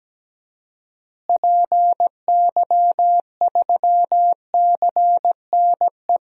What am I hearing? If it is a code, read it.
Morse code, heard as a keyed tone, about 17 wpm, 700 Hz. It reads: PY3CNE
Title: DXCC-100 Call Signs - 6 Characters - Repeated 3x - 17wpm